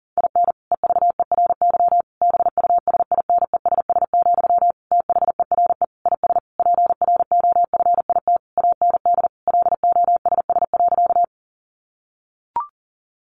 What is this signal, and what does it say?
Morse code, heard as a keyed tone, about 37 wpm, 700 Hz. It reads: IN EVERY BUSINESS, THERE IS PROFIT AND LOSS.
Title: Sentences from Top 1400 Words 40wpm